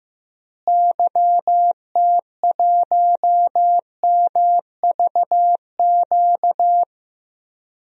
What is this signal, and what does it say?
Morse code, heard as a keyed tone, about 16 wpm, 700 Hz. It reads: YT1MVQ
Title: DXCC-100 Call Signs - 6 Characters - Repeated 3x - 15wpm